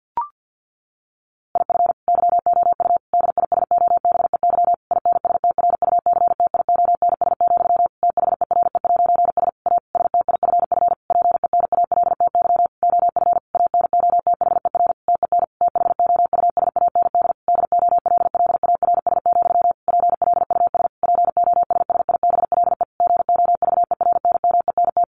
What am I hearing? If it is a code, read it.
Morse code, heard as a keyed tone, about 50 wpm, 700 Hz. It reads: IF YOU DISOBEY INSTRUCTIONS, THERE'S A STIFF PENALTY OF ANOTHER TEN THOUSAND DOLLARS, PLUS POSSIBLE GOVERNMENT
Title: A Sound Of Thunder 50wpm